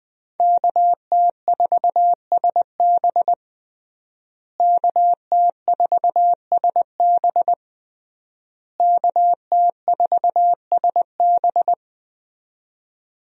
Morse code, heard as a keyed tone, about 21 wpm, 700 Hz. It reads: KT4SB KT4SB KT4SB
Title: USA Call Signs - 5 Characters - Repeated 3x - 20wpm